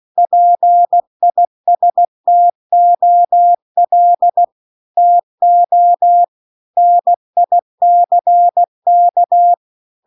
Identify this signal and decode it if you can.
Morse code, heard as a keyed tone, about 16 wpm, 700 Hz. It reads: PISTOL TO NICK